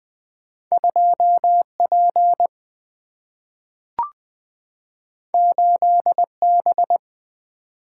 Morse code, heard as a keyed tone, about 20 wpm, 700 Hz. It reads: 2P 8B
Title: Single-Letter Single-Number 20wpm